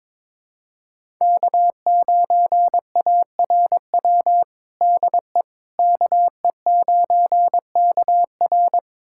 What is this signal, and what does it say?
Morse code, heard as a keyed tone, about 21 wpm, 700 Hz. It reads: K9ARW DE KE9KR
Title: USA Callsign DE Callsign - 5 Characters - 22wpm